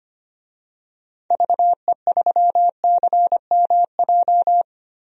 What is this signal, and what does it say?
Morse code, heard as a keyed tone, about 25 wpm, 700 Hz. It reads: VE3CMJ
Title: DXCC-100 Call Signs - 6 Characters - 25wpm